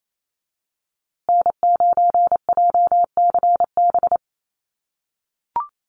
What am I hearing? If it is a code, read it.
Morse code, heard as a keyed tone, about 28 wpm, 700 Hz. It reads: N9JCB
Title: USA Call Signs Repeated 2x 28wpm